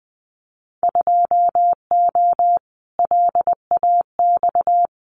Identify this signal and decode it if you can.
Morse code, heard as a keyed tone, about 20 wpm, 700 Hz. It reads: 2O LAX
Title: Winter Field Day Exchanges 20wpm